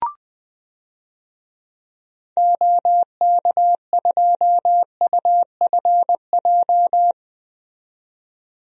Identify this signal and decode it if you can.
Morse code, heard as a keyed tone, about 21 wpm, 700 Hz. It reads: OK2UFJ
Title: DXCC-100 Call Signs - 6 Characters - 20wpm